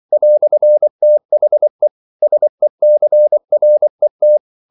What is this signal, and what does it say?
Morse code, heard as a keyed tone, about 24 wpm, 600 Hz. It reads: "THE SECRET
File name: CW / Practice / EnglishQuotations0224